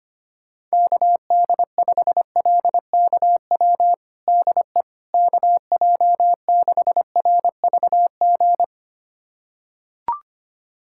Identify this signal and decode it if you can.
Morse code, heard as a keyed tone, about 25 wpm, 700 Hz. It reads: KD5LKW DE KJ6RVG
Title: USA Callsign DE Callsign - 6 Characters - 25wpm